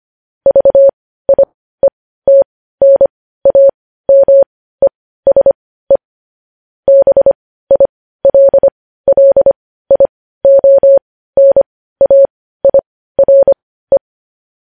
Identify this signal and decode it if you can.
Morse code, heard as a keyed tone, about 25 wpm, spaced as for 15 wpm, 550 Hz. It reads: VIETNAMESE BILLIONAIRE
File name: podcast / News headlines-1x-25x15